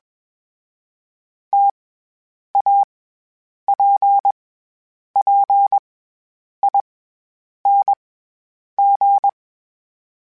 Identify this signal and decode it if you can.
Morse code, heard as a keyed tone, about 21 wpm, 800 Hz. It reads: T A P P I N G
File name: Variable Speed / Variable Speed-mp3s / T AOEI's sp @20